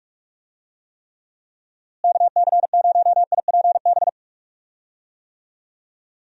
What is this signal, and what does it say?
Morse code, heard as a keyed tone, about 46 wpm, 700 Hz. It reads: KC0IPB
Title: USA Call Signs - 6 Characters - Repeated 2x - 45wpm